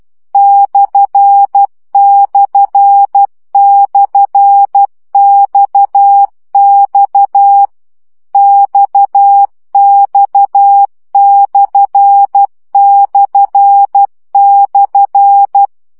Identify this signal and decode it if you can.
Morse code, heard as a keyed tone, about 12 wpm, 800 Hz. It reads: ///XX XX///
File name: CW gyakorlas x per 10 baud